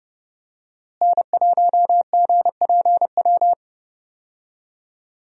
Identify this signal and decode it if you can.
Morse code, heard as a keyed tone, about 30 wpm, 700 Hz. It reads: N1GPW
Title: USA Call Signs - 5 Characters - Repeated 2x - 30wpm